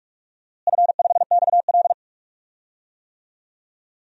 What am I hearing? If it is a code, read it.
Morse code, heard as a keyed tone, about 46 wpm, 700 Hz. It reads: F5XL
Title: DXCC-100 Call Signs - 4 Characters - Repeated 3x - 45wpm